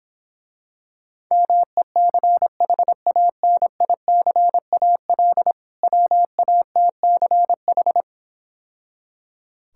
Morse code, heard as a keyed tone, about 26 wpm, 700 Hz. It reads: MECHANICAL WATCH